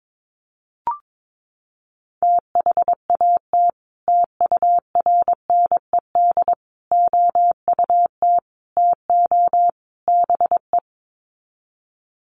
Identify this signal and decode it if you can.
Morse code, heard as a keyed tone, about 22 wpm, 700 Hz. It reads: THAT TURNED OUT TO BE
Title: Sets of 5 Words Encore2 22wpm